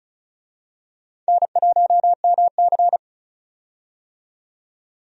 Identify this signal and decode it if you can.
Morse code, heard as a keyed tone, about 34 wpm, 700 Hz. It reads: N1MC
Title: USA Call Signs - 4 Characters - 35wpm